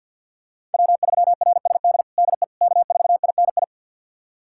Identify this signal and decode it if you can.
Morse code, heard as a keyed tone, about 46 wpm, 700 Hz. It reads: W3RSD DE K4INI
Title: USA Callsign DE Callsign - 5 Characters - 50wpm